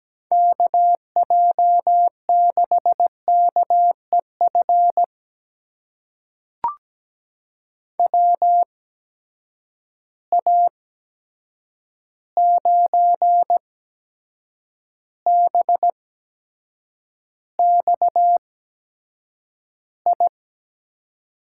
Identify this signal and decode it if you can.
Morse code, heard as a keyed tone, about 17 wpm, spaced as for 7 wpm, 700 Hz. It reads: KJ6KEF WA9BXI
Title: USA Call Signs - 6 Characters - 8x Character Spacing - 17wpm